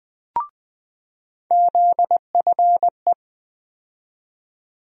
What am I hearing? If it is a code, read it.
Morse code, heard as a keyed tone, about 20 wpm, 700 Hz. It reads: ZFE
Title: USA Call Sign Postfixes - 3 Letters 20wpm